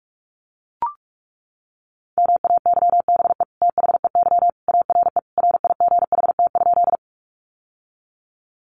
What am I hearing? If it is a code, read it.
Morse code, heard as a keyed tone, about 44 wpm, 700 Hz. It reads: MAYBE THEY ARE RIGHT?
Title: Sentences from Top 1000 Words 45wpm